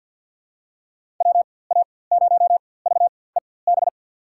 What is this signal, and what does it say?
Morse code, heard as a keyed tone, about 52 wpm, spaced as for 23 wpm, 700 Hz. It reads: WA0VEB
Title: USA Call Signs - 6 Characters - 4x Character Spacing - 50wpm